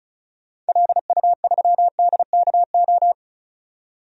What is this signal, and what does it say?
Morse code, heard as a keyed tone, about 34 wpm, 700 Hz. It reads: LU3DKO
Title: DXCC-100 Call Signs - 6 Characters - 35wpm